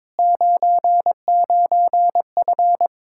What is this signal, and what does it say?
Morse code, heard as a keyed tone, about 22 wpm, 700 Hz. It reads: 99F
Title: Speed-Racing Common QSO Phrases Encore - 22wpm